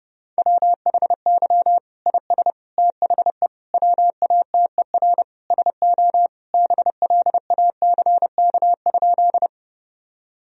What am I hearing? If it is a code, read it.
Morse code, heard as a keyed tone, about 30 wpm, 700 Hz. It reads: WHY IS THE WATER SO BLACK?
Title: Sentences from Top 1000 Words - Review Part 2 - 30wpm